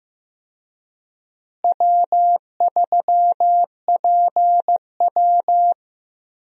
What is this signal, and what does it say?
Morse code, heard as a keyed tone, about 15 wpm, 700 Hz. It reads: W3PW
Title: USA Call Signs - 4 Characters - Repeated 3x - 15wpm